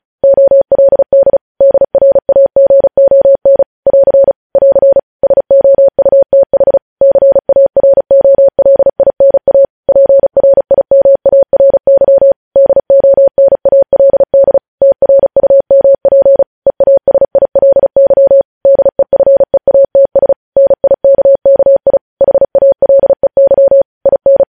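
Morse code, heard as a keyed tone, about 36 wpm, 550 Hz. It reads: OLD DRAGON <AR> <AR> SOUTH CAROLINA PRIMARY DONALD TRUMP EASILY DEFEATS NIKKI HALEY IN
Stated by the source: Morse code transmission of news headlines, preceded by vvv.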